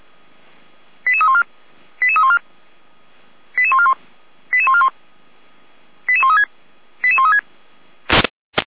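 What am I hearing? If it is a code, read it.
Five-tone selective call (ZVEI), heard as quick runs of five tones, tones of about 70 ms. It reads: 88314 88314 88141 88141 88136 88136
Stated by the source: ZVEI five-tone call received by SDR.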